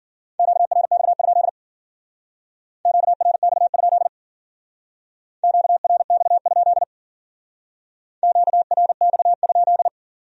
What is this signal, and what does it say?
Morse code, heard as a keyed tone, about 49 wpm, 700 Hz. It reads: QRX? QRX? QRX? QRX?
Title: Speed-Racing Q-Codes - 40wpm